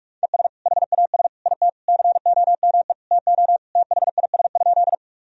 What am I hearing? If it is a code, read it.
Morse code, heard as a keyed tone, about 42 wpm, 700 Hz. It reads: ES HAS IT COME TO THIS?
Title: Sentences from Top 100 Words and Abbreviations 45wpm